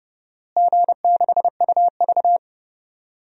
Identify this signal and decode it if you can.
Morse code, heard as a keyed tone, about 32 wpm, 700 Hz. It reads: G6UV
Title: DXCC-100 Call Signs - 4 Characters - Repeated 3x - 30wpm